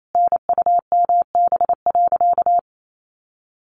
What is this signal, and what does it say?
Morse code, heard as a keyed tone, about 28 wpm, 700 Hz. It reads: NUMB.
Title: A Sound Of Thunder 28wpm